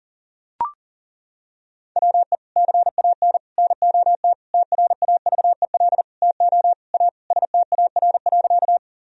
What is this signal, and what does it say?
Morse code, heard as a keyed tone, about 40 wpm, 700 Hz. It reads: WE CAN NOT TRAVEL TO A STAR.
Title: Sentences from Top 1000 Words - Review Part 2 - 40wpm